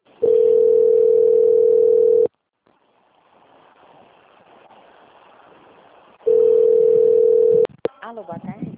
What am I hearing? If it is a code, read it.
Ringback tone (North America), 440 + 480 Hz, with speech.
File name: out- - - - - .38925